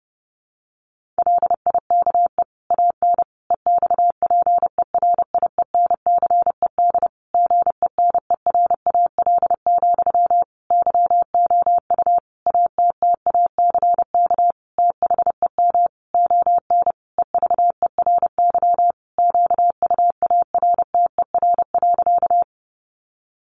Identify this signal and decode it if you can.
Morse code, heard as a keyed tone, about 30 wpm, 700 Hz. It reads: LIKE AN EXPERIENCED GENERAL, YOU ATTACK THEM ON EVERY QUARTER.
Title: Sentences from Top 1500 Words 30wpm